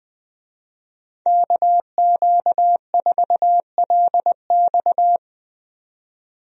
Morse code, heard as a keyed tone, about 20 wpm, 700 Hz. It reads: KQ4LX
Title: USA Call Signs - 5 Characters - 20wpm